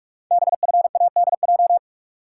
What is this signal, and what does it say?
Morse code, heard as a keyed tone, about 44 wpm, 690 Hz. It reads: BFADJ
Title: Code Groups - 5 Letters - 45wpm